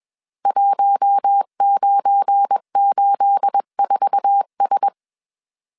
Morse code, heard as a keyed tone, about 21 wpm, 800 Hz. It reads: 1984S
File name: CW / DET / 1 9 1500C S C 20wpm 2310